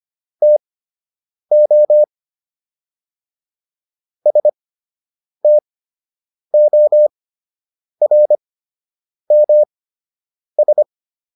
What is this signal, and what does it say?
Morse code, heard as a keyed tone, about 25 wpm, spaced as for 8 wpm, 600 Hz. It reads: TO STORMS